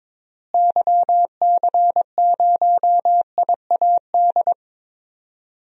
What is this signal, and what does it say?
Morse code, heard as a keyed tone, about 22 wpm, 700 Hz. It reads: YC0IAD
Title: DXCC-100 Call Signs - 6 Characters - Repeated 2x - 22wpm